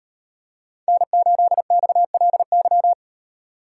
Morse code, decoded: N8XLY

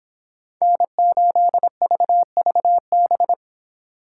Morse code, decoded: N8VVB

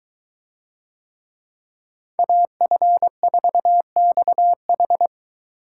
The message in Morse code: AF4XH

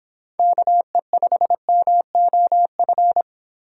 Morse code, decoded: KE5MOF

